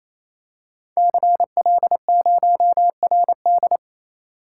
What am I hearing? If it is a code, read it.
Morse code, heard as a keyed tone, about 27 wpm, 700 Hz. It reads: CL0RD
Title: Code Groups - 5 Letter:Numbers - 28wpm